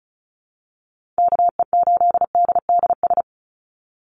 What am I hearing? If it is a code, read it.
Morse code, heard as a keyed tone, about 34 wpm, 700 Hz. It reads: KE8DDS
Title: USA Call Signs 35wpm